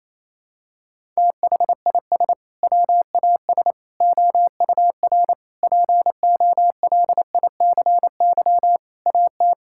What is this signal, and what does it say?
Morse code, heard as a keyed tone, about 28 wpm, 700 Hz. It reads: THIS WAS OUR POLICY AT
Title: Sentences from Top 1000 Words - Review Part 5 - 28wpm